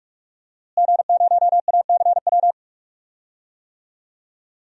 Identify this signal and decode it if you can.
Morse code, heard as a keyed tone, about 43 wpm, 700 Hz. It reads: G0ACW
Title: DXCC-100 Call Signs - 5 Characters - Repeated 3x - 45wpm